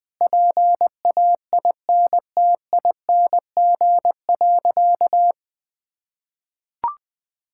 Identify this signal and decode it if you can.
Morse code, heard as a keyed tone, about 20 wpm, 700 Hz. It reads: PAINTING.
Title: Sentences from Top 1000 Words - Review Part 6 - 20wpm